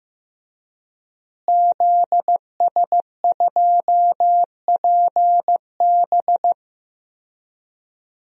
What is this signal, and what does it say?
Morse code, heard as a keyed tone, about 15 wpm, 700 Hz. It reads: ZS2PB
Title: DXCC-100 Call Signs - 5 Characters - Repeated 2x - 15wpm